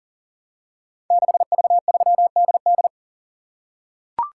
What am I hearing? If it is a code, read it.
Morse code, heard as a keyed tone, about 41 wpm, 700 Hz. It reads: 6V3DD
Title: Code Groups - 5 Letter:Numbers - 40wpm